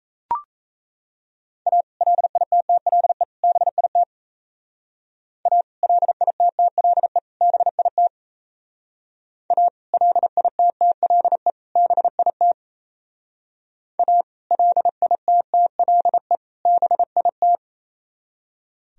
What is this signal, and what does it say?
Morse code, heard as a keyed tone, about 36 wpm, 700 Hz. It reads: A LITTLE BIT A LITTLE BIT A LITTLE BIT A LITTLE BIT